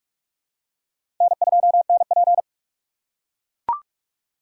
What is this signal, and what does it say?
Morse code, heard as a keyed tone, about 45 wpm, 700 Hz. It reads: N2NP
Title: USA Call Signs - 4 Characters - 4x Character Spacing - 45wpm